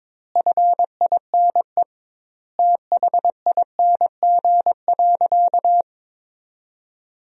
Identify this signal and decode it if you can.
Morse code, heard as a keyed tone, about 22 wpm, 700 Hz. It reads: FINE THING.
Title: 2x Word-Spacing - Sentences from Top 500 Words - 22wpm